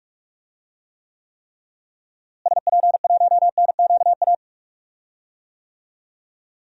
Morse code, heard as a keyed tone, about 43 wpm, 700 Hz. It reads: SP1NQA
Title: DXCC-100 Call Signs - 6 Characters - Repeated 3x - 45wpm